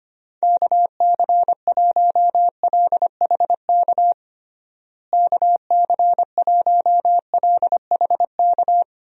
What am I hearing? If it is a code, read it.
Morse code, heard as a keyed tone, about 25 wpm, 700 Hz. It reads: KC1LHK KC1LHK
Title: USA Call Signs - 6 Characters - Repeated 3x - 25wpm